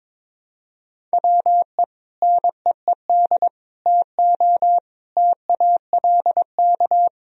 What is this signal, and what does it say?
Morse code, heard as a keyed tone, about 22 wpm, 700 Hz. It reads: WE NEED TO TALK
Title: Sentences from Top 1000 Words - Review Part 4 - 22wpm